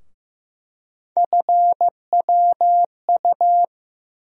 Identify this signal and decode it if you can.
Morse code, heard as a keyed tone, about 15 wpm, 700 Hz. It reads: FWU
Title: USA Call Sign Postfixes - 3 Letters 15wpm